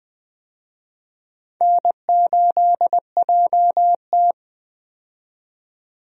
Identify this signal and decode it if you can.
Morse code, heard as a keyed tone, about 20 wpm, 700 Hz. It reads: N8JT
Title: USA Call Signs - 4 Characters - 20wpm